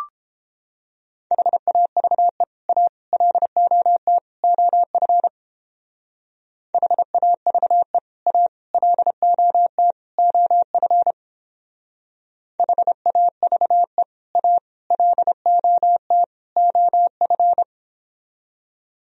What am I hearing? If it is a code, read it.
Morse code, heard as a keyed tone, about 31 wpm, 700 Hz. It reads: HAVE A LOT OF HAVE A LOT OF HAVE A LOT OF